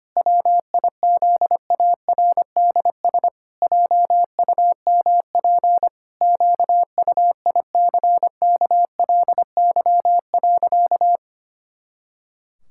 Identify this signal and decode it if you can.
Morse code, heard as a keyed tone, about 25 wpm, 700 Hz. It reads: WIZARDS JUMP QUICKLY.
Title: Pangram Challenge v2 - 25wpm